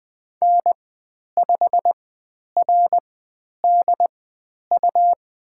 Morse code, heard as a keyed tone, about 20 wpm, spaced as for 10 wpm, 700 Hz. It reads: N5RDU
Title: 109 Lesson 18g - Letter m - Call Signs 20wpm